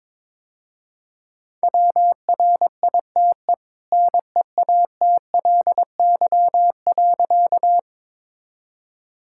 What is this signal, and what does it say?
Morse code, heard as a keyed tone, about 22 wpm, 690 Hz. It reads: WRITE NEATLY.
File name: practice / 22wpm / Easy Sentences with Hints - 22wpm